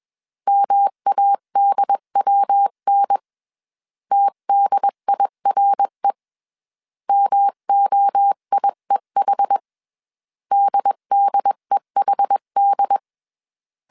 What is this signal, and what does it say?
Morse code, heard as a keyed tone, about 21 wpm, 800 Hz. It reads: MADWN TDIRE MOIEH DDEHD